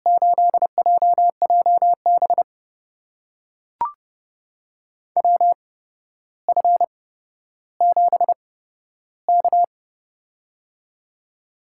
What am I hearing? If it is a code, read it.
Morse code, heard as a keyed tone, about 30 wpm, spaced as for 13 wpm, 700 Hz. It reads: 8JJB WF7K
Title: USA Call Signs - 8x Character Spacing - 30wpm